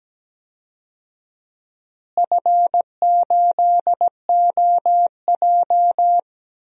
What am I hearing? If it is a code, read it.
Morse code, heard as a keyed tone, about 17 wpm, 700 Hz. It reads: F8OJ
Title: DXCC-100 Call Signs - 4 Characters - Repeated 3x - 17wpm